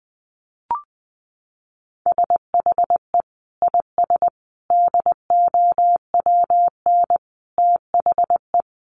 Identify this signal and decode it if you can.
Morse code, heard as a keyed tone, about 20 wpm, 700 Hz. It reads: SHE IS DOWN THE
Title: Sentences from Top 1400 Words 20wpm